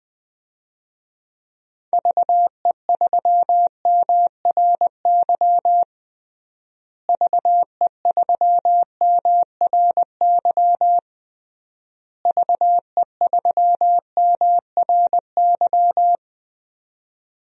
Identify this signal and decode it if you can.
Morse code, heard as a keyed tone, about 20 wpm, 700 Hz. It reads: VE3MRY VE3MRY VE3MRY
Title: DXCC-100 Call Signs - 6 Characters - Repeated 3x - 20wpm